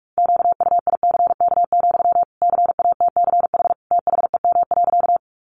Morse code, heard as a keyed tone, about 44 wpm, 700 Hz. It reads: QUICK, CATCH THEM.
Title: Sentences from Top 1500 Words 45wpm